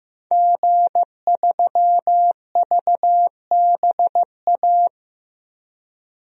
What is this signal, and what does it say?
Morse code, heard as a keyed tone, about 15 wpm, 700 Hz. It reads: G3VBA
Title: DXCC-100 Call Signs - 5 Characters - Repeated 3x - 15wpm